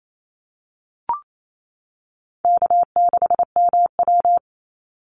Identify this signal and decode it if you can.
Morse code, heard as a keyed tone, about 28 wpm, 700 Hz. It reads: K6MW
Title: NA SOTA Chaser Callsigns 28wpm